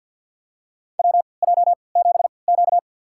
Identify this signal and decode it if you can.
Morse code, heard as a keyed tone, about 52 wpm, spaced as for 28 wpm, 700 Hz. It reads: WJ7Q
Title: USA Call Signs - 4 Characters - 3x Character Spacing - 50wpm